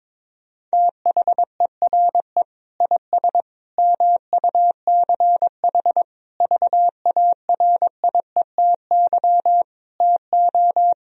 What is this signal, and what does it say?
Morse code, heard as a keyed tone, about 22 wpm, 700 Hz. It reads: THERE IS MUCH VARIETY TO
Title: Sentences from Top 1000 Words - Review Part 4 - 22wpm